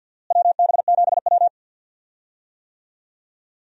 Morse code, heard as a keyed tone, about 47 wpm, 700 Hz. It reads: WB7W
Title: USA Call Signs - 4 Characters - Repeated 3x - 50wpm